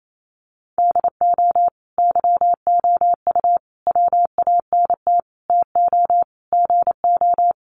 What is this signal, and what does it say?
Morse code, heard as a keyed tone, about 28 wpm, 700 Hz. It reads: DO YOU WANT TO GO
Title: Sentences from Top 1400 Words 28wpm